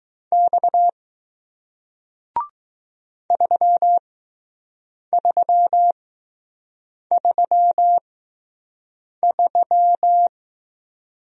Morse code, heard as a keyed tone, about 20 wpm, spaced as for 6 wpm, 700 Hz. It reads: X 3333